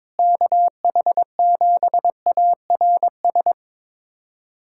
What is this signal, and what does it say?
Morse code, heard as a keyed tone, about 22 wpm, 700 Hz. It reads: KH7ARS
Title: USA Callsign DE Callsign - 6 Characters - 22wpm